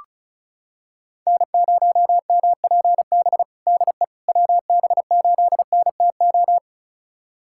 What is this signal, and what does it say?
Morse code, heard as a keyed tone, about 36 wpm, 700 Hz. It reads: N0MPB DE WB8NTO
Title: USA Callsign DE Callsign 35wpm